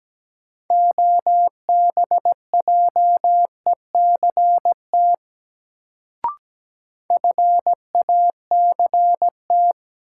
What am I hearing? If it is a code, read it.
Morse code, heard as a keyed tone, about 17 wpm, 700 Hz. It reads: OBJECT FACT
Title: Sentences from Top 500 Words - New Word Drill - 17wpm